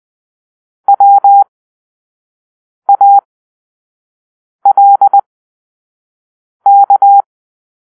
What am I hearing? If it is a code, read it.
Morse code, heard as a keyed tone, about 20 wpm, spaced as for 5 wpm, 800 Hz. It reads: WALK
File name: arrl / Grp-5 Drill-3